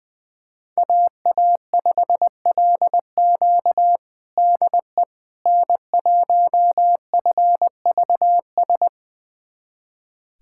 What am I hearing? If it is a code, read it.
Morse code, heard as a keyed tone, about 20 wpm, 700 Hz. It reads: AA5LQ DE N1FVS
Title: USA Callsign DE Callsign - 5 Characters - 20wpm